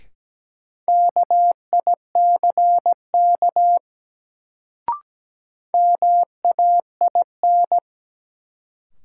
Morse code, heard as a keyed tone, about 17 wpm, 700 Hz. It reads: KICK MAIN
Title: Sentences from Top 1300 Words - New Word Drill Alt - 17wpm